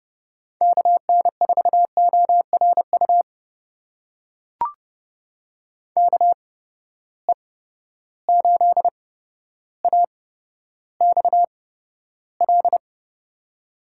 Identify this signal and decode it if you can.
Morse code, heard as a keyed tone, about 31 wpm, spaced as for 13 wpm, 700 Hz. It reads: KN4ORU KE8AXL
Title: USA Call Signs - 6 Characters - 8x Character Spacing - 30wpm